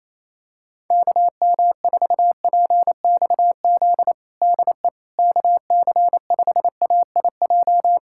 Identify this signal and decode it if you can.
Morse code, heard as a keyed tone, about 28 wpm, 700 Hz. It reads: KM4PXZ DE KC5AIJ
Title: USA Callsign DE Callsign - 6 Characters - 28wpm